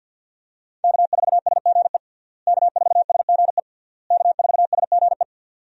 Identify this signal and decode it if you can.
Morse code, heard as a keyed tone, about 52 wpm, 700 Hz. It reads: K4SGE K4SGE K4SGE
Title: USA Call Signs - 5 Characters - Repeated 3x - 50wpm